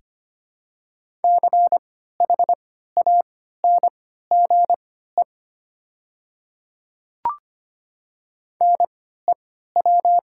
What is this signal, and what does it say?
Morse code, heard as a keyed tone, about 25 wpm, spaced as for 14 wpm, 700 Hz. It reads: CHANGE NEW
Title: 3x Character Spacing - Top 200 Words - 25wpm